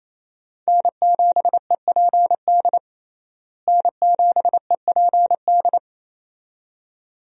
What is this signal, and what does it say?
Morse code, heard as a keyed tone, about 28 wpm, 700 Hz. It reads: N7EPD N7EPD
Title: USA Call Signs - 5 Characters - Repeated 2x - 28wpm